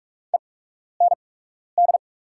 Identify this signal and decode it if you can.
Morse code, heard as a keyed tone, about 44 wpm, spaced as for 12 wpm, 690 Hz. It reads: END
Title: 8x Character Spacing - Top 200 Words - 45wpm